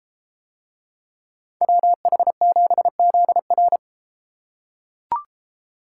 Morse code, heard as a keyed tone, about 33 wpm, 700 Hz. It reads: WH7ZR